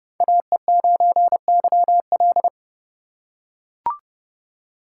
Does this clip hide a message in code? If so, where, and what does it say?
Yes: Morse code from 0.2 to 2.5 s, heard as a keyed tone, about 28 wpm, 700 Hz. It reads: AE9YL